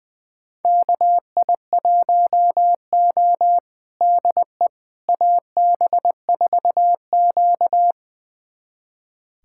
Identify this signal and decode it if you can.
Morse code, heard as a keyed tone, about 21 wpm, 700 Hz. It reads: KI1O DE AB4Q
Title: USA Callsign DE Callsign - 4 Characters - 20wpm